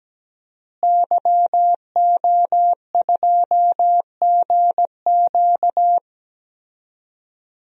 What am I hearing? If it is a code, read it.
Morse code, heard as a keyed tone, about 17 wpm, 700 Hz. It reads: YO2GQ
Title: DXCC-100 Call Signs - 5 Characters - Repeated 2x - 17wpm